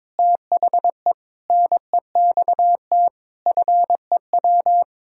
Morse code, heard as a keyed tone, about 21 wpm, 700 Hz. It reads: THE NEXT FEW
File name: practice / 22wpm / Sets of 7 Words - 22wpm